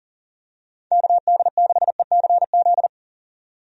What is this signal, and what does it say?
Morse code, heard as a keyed tone, about 40 wpm, 700 Hz. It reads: KD6ECZ
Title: USA Call Signs - 6 Characters - 40wpm